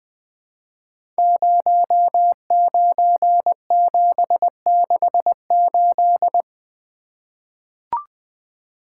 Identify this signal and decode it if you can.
Morse code, heard as a keyed tone, about 20 wpm, 700 Hz. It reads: 09768
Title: Code Groups - 5 Letter:Numbers - 20wpm